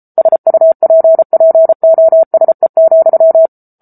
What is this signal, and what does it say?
Morse code, heard as a keyed tone, about 34 wpm, 650 Hz. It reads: SUPPOSE,